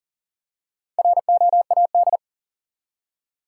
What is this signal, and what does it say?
Morse code, heard as a keyed tone, about 40 wpm, 700 Hz. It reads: ROAD
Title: Four Letter Words - 40wpm